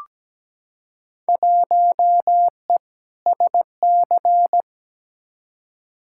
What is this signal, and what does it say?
Morse code, heard as a keyed tone, about 17 wpm, 700 Hz. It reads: 1E SC
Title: Field Day Exchanges - 17wpm